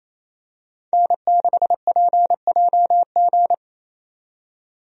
Morse code, heard as a keyed tone, about 28 wpm, 700 Hz. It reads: N6PJG
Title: USA Call Signs - 5 Characters - 28wpm